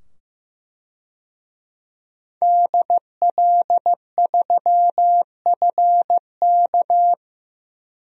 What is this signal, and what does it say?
Morse code, heard as a keyed tone, about 15 wpm, 700 Hz. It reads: DL3FK
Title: DXCC-100 Call Signs - 5 Characters - 15wpm